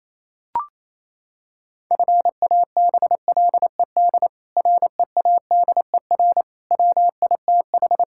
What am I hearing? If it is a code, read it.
Morse code, heard as a keyed tone, about 28 wpm, 700 Hz. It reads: FABLED READER WITH
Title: Pangram Challenge v2 - 28wpm